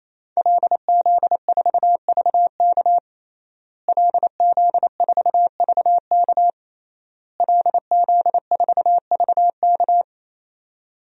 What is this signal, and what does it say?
Morse code, heard as a keyed tone, about 28 wpm, 700 Hz. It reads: LZ4VK LZ4VK LZ4VK
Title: DXCC-100 Call Signs - 5 Characters - Repeated 3x - 28wpm